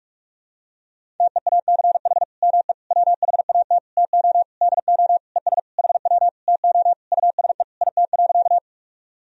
Morse code, heard as a keyed tone, about 42 wpm, 700 Hz. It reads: TEACH ME WHAT TO DO ES HW TO USE IT.
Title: Sentences from Top 500 Words and Abbreviations 45wpm